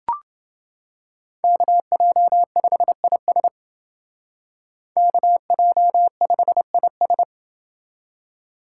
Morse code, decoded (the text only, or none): KJ5IS KJ5IS